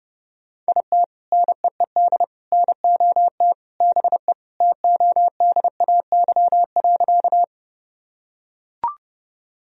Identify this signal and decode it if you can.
Morse code, heard as a keyed tone, about 30 wpm, 700 Hz. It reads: IT NEED NOT BE TODAY.
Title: Sentences from Top 1000 Words - Review Part 4 - 30wpm